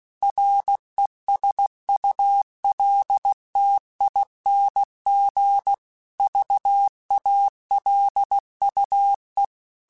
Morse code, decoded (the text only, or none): RESULTING VALUE